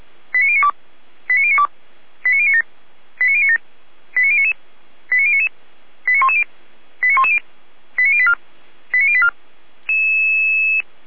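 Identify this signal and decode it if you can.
Five-tone selective call (ZVEI), heard as quick runs of five tones, tones of about 70 ms. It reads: 89092 89092 89097 89097 89099 89099 89110 89110 89084 89084
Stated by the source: ZVEI five-tone selective call received on an SDR.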